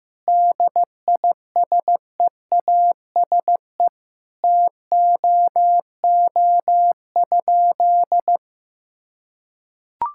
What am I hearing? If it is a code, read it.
Morse code, heard as a keyed tone, about 15 wpm, 700 Hz. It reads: DISEASE TOO?
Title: Sentences from Top 1000 Words - Review Part 6 - 15wpm